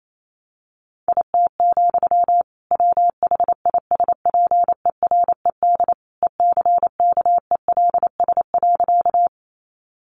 Morse code, heard as a keyed tone, about 28 wpm, 700 Hz. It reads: IT, WHISPERED ECKELS.
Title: A Sound Of Thunder 28wpm